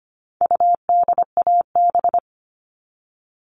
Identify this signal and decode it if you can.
Morse code, heard as a keyed tone, about 25 wpm, 700 Hz. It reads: UDAB
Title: All Cut Numbers - 4-Digit 25wpm